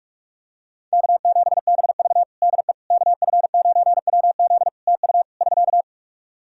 Morse code, heard as a keyed tone, about 46 wpm, 700 Hz. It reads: K7BV DE KF9WZ TU <SK>
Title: K1USN SST QSO Practice 45wpm